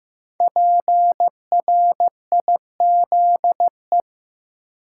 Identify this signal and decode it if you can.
Morse code, heard as a keyed tone, about 15 wpm, 700 Hz. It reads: PRIZE
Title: Five Letter Words - 15wpm